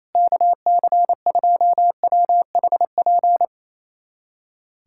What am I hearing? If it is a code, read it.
Morse code, heard as a keyed tone, about 28 wpm, 700 Hz. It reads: KC2WHP
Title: USA Callsign DE Callsign - 6 Characters - 28wpm